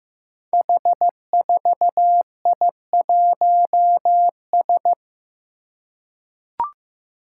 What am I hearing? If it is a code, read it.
Morse code, heard as a keyed tone, about 15 wpm, 700 Hz. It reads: H4I1S
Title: Five Challenging Letter-Number Combinations 15wpm